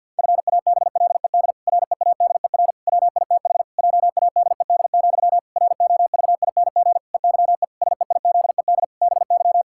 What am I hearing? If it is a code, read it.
Morse code, heard as a keyed tone, about 50 wpm, 700 Hz. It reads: FABLED READER WITH JADED, ROVING EYE SEIZED BY
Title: Pangram Challenge v2 - 50wpm